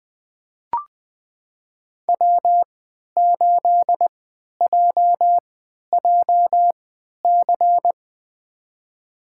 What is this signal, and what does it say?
Morse code, heard as a keyed tone, about 20 wpm, spaced as for 11 wpm, 700 Hz. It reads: W8JJC
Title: USA Call Signs - 3x Character Spacing - 20wpm